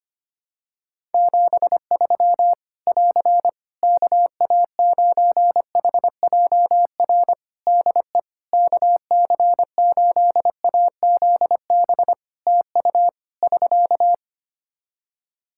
Morse code, heard as a keyed tone, about 25 wpm, 700 Hz. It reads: 73 <AR> KA9HJR DE KC8AZB TU <SK>
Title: Practice QSO - Exchange 4 - Ending - 25wpm